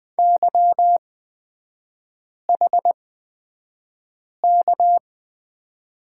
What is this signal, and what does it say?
Morse code, heard as a keyed tone, about 20 wpm, spaced as for 5 wpm, 690 Hz. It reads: YHK
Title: 229 Lesson 37e - Letter x - Three Characters 20wpm